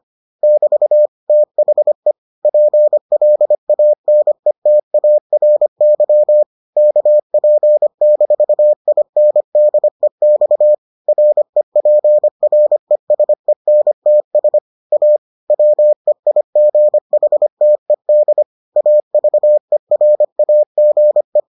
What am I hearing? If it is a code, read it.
Morse code, heard as a keyed tone, about 25 wpm, 600 Hz. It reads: <BT> THE PLANETARY KP-INDEX REPRESENTS A WEIGHTED AVERAGE